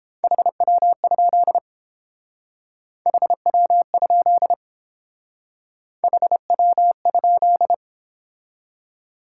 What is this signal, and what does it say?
Morse code, heard as a keyed tone, about 30 wpm, 700 Hz. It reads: HW? HW? HW?